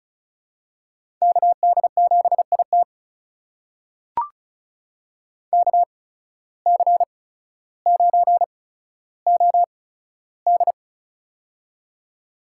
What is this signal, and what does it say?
Morse code, heard as a keyed tone, about 34 wpm, spaced as for 15 wpm, 700 Hz. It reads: KD7IT KC9OD
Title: USA Call Signs - 5 Characters - 8x Character Spacing - 35wpm